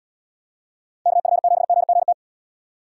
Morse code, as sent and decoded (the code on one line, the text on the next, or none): -.- ..-. --... --. --. -
KF7GGT